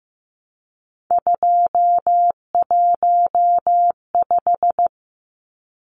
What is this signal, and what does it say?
Morse code, heard as a keyed tone, about 15 wpm, 700 Hz. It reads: 215
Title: Numbers - 3-Digit 15wpm